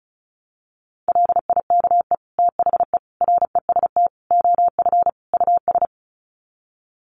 Morse code, heard as a keyed tone, about 34 wpm, 700 Hz. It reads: LIKE THE REST OF US
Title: Sets of 5 Words Encore2 35wpm